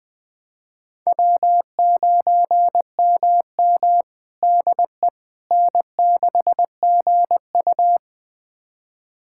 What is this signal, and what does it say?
Morse code, heard as a keyed tone, about 21 wpm, 700 Hz. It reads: W9MM DE N6GU
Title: USA Callsign DE Callsign - 4 Characters - 20wpm